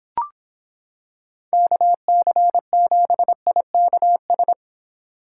Morse code, heard as a keyed tone, about 26 wpm, 700 Hz. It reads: KC7IKS